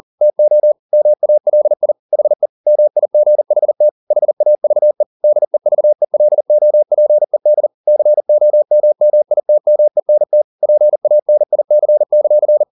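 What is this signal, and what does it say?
Morse code, heard as a keyed tone, about 40 wpm, 600 Hz. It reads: TO MALI HE MIGHT HAVE DEVELOPED COMMITMENT PANIC;
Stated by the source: Morse code at 40 wpm, practice text, CW med Gnister.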